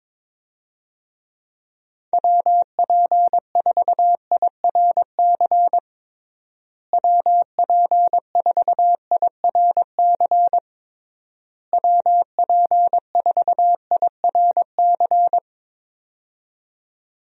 Morse code, heard as a keyed tone, about 22 wpm, 700 Hz. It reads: WP4IRC WP4IRC WP4IRC
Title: USA Call Signs - 6 Characters - Repeated 3x - 22wpm